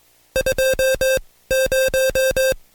Morse code, heard as a keyed tone, about 22 wpm, spaced as for 16 wpm, 530 Hz. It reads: 20
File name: Code Quick / AudioDownld2 / Seg134-Stepper7 15WPM